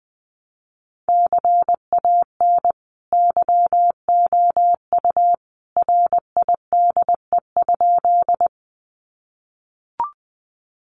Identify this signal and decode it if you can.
Morse code, heard as a keyed tone, about 20 wpm, 700 Hz. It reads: CAN YOU RIDE?
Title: Sentences from Top 1200 Words 20wpm